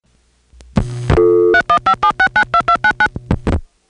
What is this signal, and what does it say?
DTMF, heard as telephone touch tones, tones of about 80 ms. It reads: 3157692299